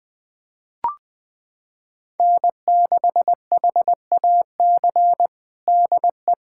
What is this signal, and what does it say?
Morse code, heard as a keyed tone, about 20 wpm, 700 Hz. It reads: N6HAC DE
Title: USA Callsign DE Callsign - 5 Characters - 20wpm